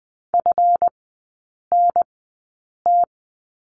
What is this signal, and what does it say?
Morse code, heard as a keyed tone, about 20 wpm, spaced as for 8 wpm, 700 Hz. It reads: FNT
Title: 159 Lesson 26e - Letter g - Three Characters 20wpm